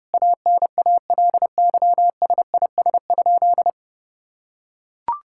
Morse code, heard as a keyed tone, about 30 wpm, 700 Hz. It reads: ANALYSIS?
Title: Sentences from Top 1000 Words - Review Part 5 - 30wpm